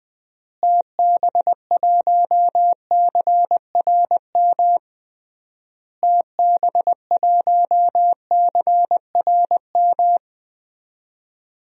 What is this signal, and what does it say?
Morse code, heard as a keyed tone, about 20 wpm, 700 Hz. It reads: TB1CRM TB1CRM
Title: DXCC-100 Call Signs - 6 Characters - Repeated 3x - 20wpm